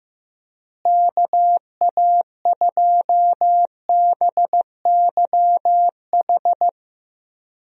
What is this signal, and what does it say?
Morse code, heard as a keyed tone, about 15 wpm, 700 Hz. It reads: KA2BYH
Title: USA Call Signs - 6 Characters - Repeated 2x - 15wpm